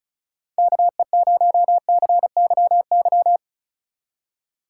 Morse code, heard as a keyed tone, about 44 wpm, 700 Hz. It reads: KE0CYY